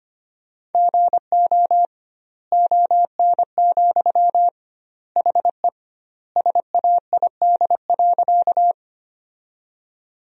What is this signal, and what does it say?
Morse code, heard as a keyed tone, about 25 wpm, 700 Hz. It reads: GO ON, HE SAID.
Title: 2x Word-Spacing - Sentences from Top 100 Words - 25wpm